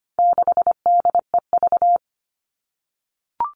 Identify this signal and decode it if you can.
Morse code, heard as a keyed tone, about 25 wpm, 700 Hz. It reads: 6DEV
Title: All Cut Numbers - 4-Digit 25wpm